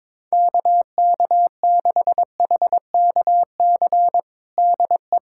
Morse code, heard as a keyed tone, about 22 wpm, 700 Hz. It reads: KK6HKC DE
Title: USA Callsign DE Callsign - 6 Characters - 22wpm